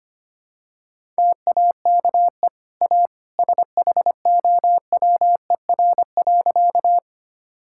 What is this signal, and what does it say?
Morse code, heard as a keyed tone, about 25 wpm, 700 Hz. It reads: TAKE A SHOWER.
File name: Easy Sentences with Hints - 25wpm